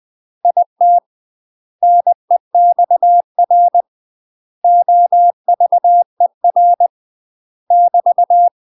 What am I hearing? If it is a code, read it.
Morse code, heard as a keyed tone, about 20 wpm, 700 Hz. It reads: IT NEXR OVER <BT>
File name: UK MASTER QSOs TRAINING / 20WPM / UK MASTER QSOs 0035